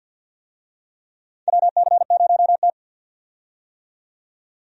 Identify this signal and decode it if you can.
Morse code, heard as a keyed tone, about 53 wpm, 700 Hz. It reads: WC0T
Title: USA Call Signs - 4 Characters - 50wpm